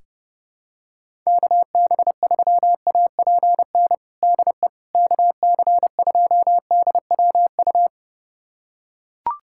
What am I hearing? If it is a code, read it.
Morse code, heard as a keyed tone, about 32 wpm, 700 Hz. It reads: KB3APN DE KC2DWU
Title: USA Callsign DE Callsign - 6 Characters - 30wpm